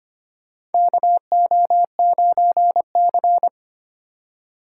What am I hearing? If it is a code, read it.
Morse code, heard as a keyed tone, about 25 wpm, 700 Hz. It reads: KO9C
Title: USA Call Signs - 4 Characters - 2x Character Spacing - 25wpm